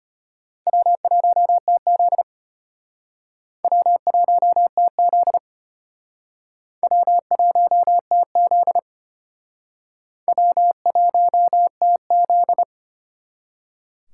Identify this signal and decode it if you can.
Morse code, heard as a keyed tone, about 32 wpm, 700 Hz. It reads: W1TZ W1TZ W1TZ W1TZ